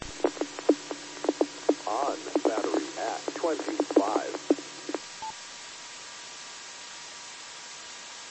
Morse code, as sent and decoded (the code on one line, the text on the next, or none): none